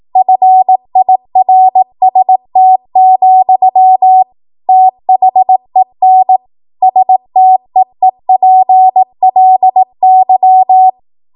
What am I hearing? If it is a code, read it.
Morse code, heard as a keyed tone, about 18 wpm, 750 Hz. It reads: FIRST, THEN STEEPLY
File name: Archive / 18 WPM / 18WPM